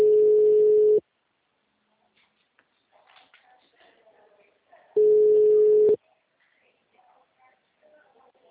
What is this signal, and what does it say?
Ringback tone (Europe, 425 Hz).
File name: out- -2022- - - .12941